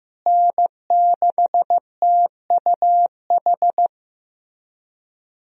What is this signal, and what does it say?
Morse code, heard as a keyed tone, about 15 wpm, 700 Hz. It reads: N6TUH
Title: USA Call Signs - 5 Characters - Repeated 3x - 15wpm